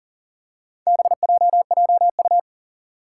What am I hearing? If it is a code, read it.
Morse code, heard as a keyed tone, about 40 wpm, 700 Hz. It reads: BJJU
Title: Four Challenging Letter-Number Combinations 40wpm